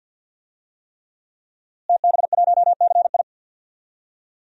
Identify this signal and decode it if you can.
Morse code, heard as a keyed tone, about 47 wpm, 700 Hz. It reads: TB1CI
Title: DXCC-100 Call Signs - 5 Characters - Repeated 3x - 50wpm